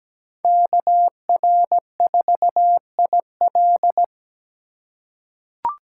Morse code, heard as a keyed tone, about 17 wpm, 700 Hz. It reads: KR4IL
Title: USA Call Signs - 5 Characters - 3x Character Spacing - 17wpm